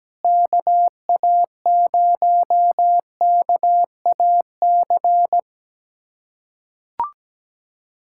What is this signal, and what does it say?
Morse code, heard as a keyed tone, about 17 wpm, 700 Hz. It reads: KA0KAC
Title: USA Callsign DE Callsign - 6 Characters - 17wpm